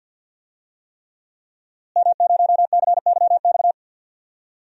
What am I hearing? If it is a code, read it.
Morse code, heard as a keyed tone, about 51 wpm, 700 Hz. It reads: M0CYX
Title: DXCC-100 Call Signs - 5 Characters - Repeated 3x - 50wpm